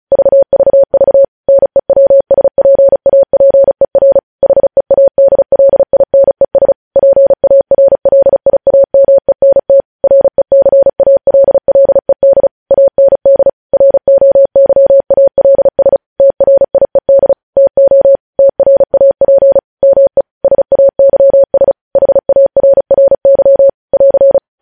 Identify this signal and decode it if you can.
Morse code, heard as a keyed tone, about 36 wpm, 550 Hz. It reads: VVV NEWSPAPER HEADLINES PARLIAMENT RECALLED AND ROYALS TRIED TO TRAP ME SAYS HARRY <AR>
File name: podcast / News headlines-1x-35x35